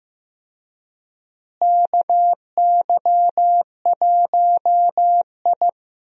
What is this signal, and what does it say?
Morse code, heard as a keyed tone, about 15 wpm, 700 Hz. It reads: KY1I
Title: USA Call Signs - 4 Characters - Repeated 2x - 15wpm